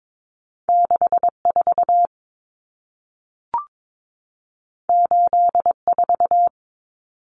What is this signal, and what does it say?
Morse code, heard as a keyed tone, about 22 wpm, 700 Hz. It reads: 64 84
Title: Numbers - 2-Digit 22wpm